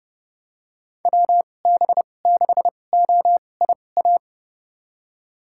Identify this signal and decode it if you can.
Morse code, heard as a keyed tone, about 30 wpm, spaced as for 22 wpm, 700 Hz. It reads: WB6OIA
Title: USA Call Signs - 2x Character Spacing - 30wpm